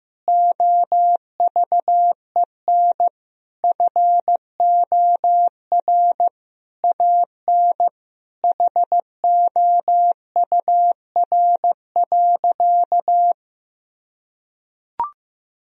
Morse code, heard as a keyed tone, about 16 wpm, 700 Hz. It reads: OVEN FOR AN HOUR.
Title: Sentences from Top 1000 Words - Review Part 4 - 15wpm